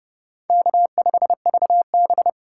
Morse code, heard as a keyed tone, about 35 wpm, 700 Hz. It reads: K5VB
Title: Speed Racing - USA 1x2 Call Signs 30wpm